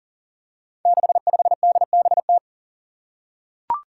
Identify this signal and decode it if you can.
Morse code, heard as a keyed tone, about 41 wpm, 700 Hz. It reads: 65DBT
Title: Five Challenging Letter-Number Combinations 40wpm